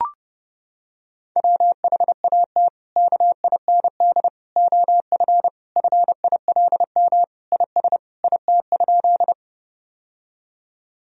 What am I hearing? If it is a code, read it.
Morse code, heard as a keyed tone, about 32 wpm, 700 Hz. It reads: WHAT KIND OF FILM IS IT?
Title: Sentences from Top 1000 Words - Review Part 3 - 30wpm